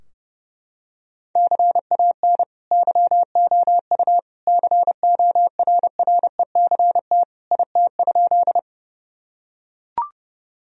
Morse code, heard as a keyed tone, about 30 wpm, 700 Hz. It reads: CAN YOU CORRECT IT?
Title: Sentences from Top 1000 Words - Review Part 2 - 30wpm